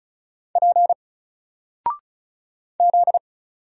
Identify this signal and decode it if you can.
Morse code, heard as a keyed tone, about 36 wpm, 700 Hz. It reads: P Z
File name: Single-Letter-Number-35wpm